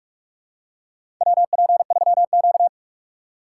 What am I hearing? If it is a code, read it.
Morse code, heard as a keyed tone, about 46 wpm, 700 Hz. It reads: WP3Q
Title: USA Call Signs - 4 Characters - 3x Character Spacing - 45wpm